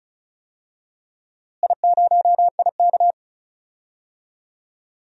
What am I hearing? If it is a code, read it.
Morse code, heard as a keyed tone, about 36 wpm, 700 Hz. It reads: I0IK
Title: DXCC-100 Call Signs - 4 Characters - Repeated 2x - 35wpm